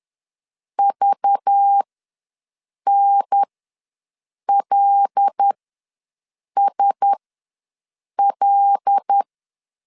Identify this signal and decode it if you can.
Morse code, heard as a keyed tone, about 10 wpm, spaced as for 6 wpm, 800 Hz. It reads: VNLSL
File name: LEZIONI / 5 LBVC / 5 LBV ws 10wpm 3645